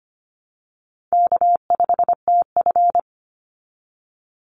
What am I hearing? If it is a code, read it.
Morse code, heard as a keyed tone, about 25 wpm, 700 Hz. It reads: K5TF
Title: Easy Call Signs Repeated 2x - 25wpm